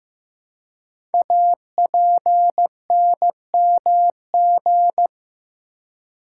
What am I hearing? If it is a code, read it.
Morse code, heard as a keyed tone, about 15 wpm, 700 Hz. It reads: APNMG
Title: Code Groups - 5 Letters - 15wpm